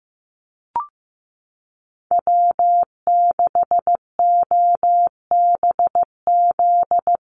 Morse code, heard as a keyed tone, about 15 wpm, 700 Hz. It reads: W6OBZ
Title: USA Call Signs 15wpm